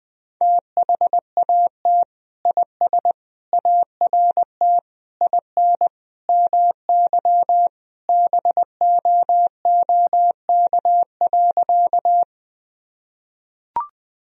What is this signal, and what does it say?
Morse code, heard as a keyed tone, about 20 wpm, 700 Hz. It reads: THAT IS ART IN MY BOOK.
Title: Sentences from Top 1000 Words - Review Part 3 - 20wpm